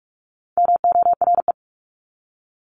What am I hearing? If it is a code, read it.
Morse code, heard as a keyed tone, about 44 wpm, 700 Hz. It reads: MORE
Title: Top 500 Words 45wpm